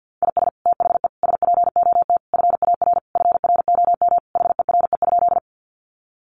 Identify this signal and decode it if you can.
Morse code, heard as a keyed tone, about 49 wpm, 700 Hz. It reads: IS THE SPOT FAR FROM HERE?
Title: Sentences from Top 1100 Words 50wpm